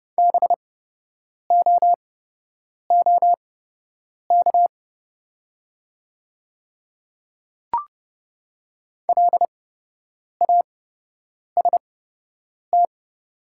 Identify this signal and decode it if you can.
Morse code, heard as a keyed tone, about 30 wpm, spaced as for 8 wpm, 700 Hz. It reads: BOOK LAST
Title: 8x Character Spacing - Top 500 Words - 30wpm